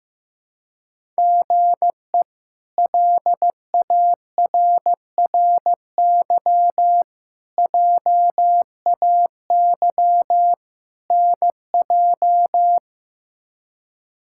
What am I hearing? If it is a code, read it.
Morse code, heard as a keyed tone, about 15 wpm, 700 Hz. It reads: GE LARRY JAY NJ
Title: K1USN SST QSO Practice - 15wpm